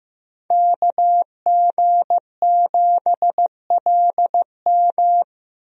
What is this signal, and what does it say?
Morse code, heard as a keyed tone, about 15 wpm, 700 Hz. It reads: KG7LM
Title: USA Call Signs - 5 Characters - Repeated 2x - 15wpm